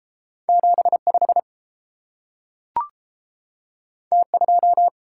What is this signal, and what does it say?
Morse code, heard as a keyed tone, about 30 wpm, 700 Hz. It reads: 75 T2